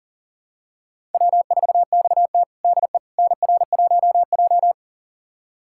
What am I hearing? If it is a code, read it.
Morse code, heard as a keyed tone, about 40 wpm, 700 Hz. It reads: W4XT DE NR1J
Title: USA Callsign DE Callsign - 4 Characters - 40wpm